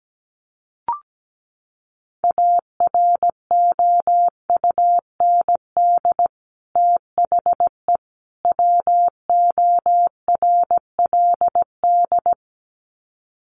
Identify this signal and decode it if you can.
Morse code, heard as a keyed tone, about 17 wpm, 700 Hz. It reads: AROUND THE WORLD
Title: Sets of 3 Words 17wpm